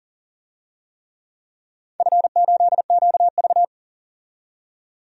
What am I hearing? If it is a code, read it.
Morse code, heard as a keyed tone, about 40 wpm, 700 Hz. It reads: F8QV
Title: DXCC-100 Call Signs - 4 Characters - 40wpm